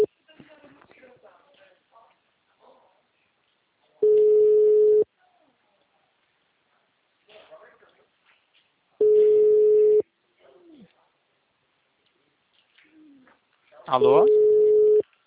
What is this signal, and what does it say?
Ringback tone (Europe, 425 Hz), with speech.
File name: out- -2011- - - .4951